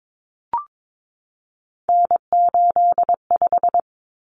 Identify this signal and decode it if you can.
Morse code, heard as a keyed tone, about 22 wpm, 700 Hz. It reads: N85
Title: Common Cut Numbers - 3-Digit 22wpm